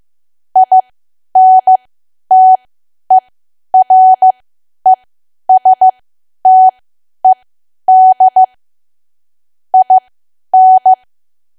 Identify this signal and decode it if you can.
Morse code, heard as a keyed tone, about 15 wpm, spaced as for 10 wpm, 750 Hz. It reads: INTERESTED IN
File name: cwa / W1AW-Level2 / 10WPM